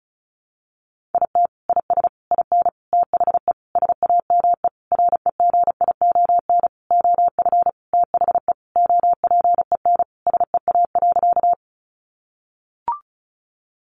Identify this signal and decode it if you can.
Morse code, heard as a keyed tone, about 34 wpm, 700 Hz. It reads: IT IS IN THE SAME REGION OF THE OPEN SEA.
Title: Sentences from Top 1500 Words 35wpm